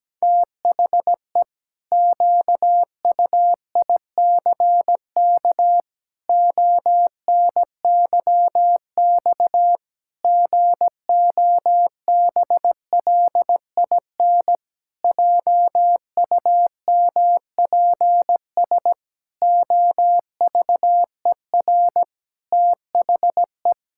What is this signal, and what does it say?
Morse code, heard as a keyed tone, about 17 wpm, 700 Hz. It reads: THE QUICK ONYX GOBLIN JUMPS OVER THE
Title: Pangram Challenge v2 - 17wpm